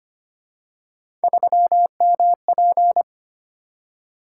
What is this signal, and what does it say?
Morse code, heard as a keyed tone, about 29 wpm, 700 Hz. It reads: 3MP
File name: Speed-Racing - Three Letter-Number Combinations - 25wpm